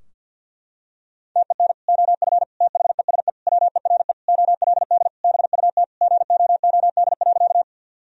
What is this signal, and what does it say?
Morse code, heard as a keyed tone, about 50 wpm, 700 Hz. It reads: TEN OF THESE WERE OLD BUT GOOD.
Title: Sentences from Top 1000 Words - Review Part 2 - 50wpm